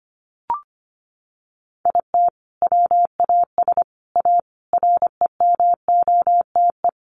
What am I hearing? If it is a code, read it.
Morse code, heard as a keyed tone, about 25 wpm, 700 Hz. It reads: IT WAS A REMOTE
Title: Sentences from Top 1300 Words 25wpm